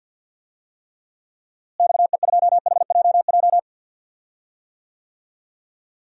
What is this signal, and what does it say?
Morse code, heard as a keyed tone, about 51 wpm, 700 Hz. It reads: XE2HJJ
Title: DXCC-100 Call Signs - 6 Characters - Repeated 3x - 50wpm